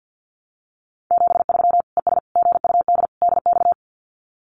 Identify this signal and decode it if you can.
Morse code, heard as a keyed tone, about 50 wpm, 700 Hz. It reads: 73 ES GUD DX
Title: Common QSO Phrases 50wpm